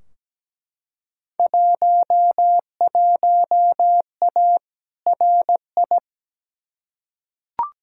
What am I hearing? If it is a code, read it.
Morse code, heard as a keyed tone, about 17 wpm, 700 Hz. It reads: 11A RI
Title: Field Day Exchanges Encore - 17wpm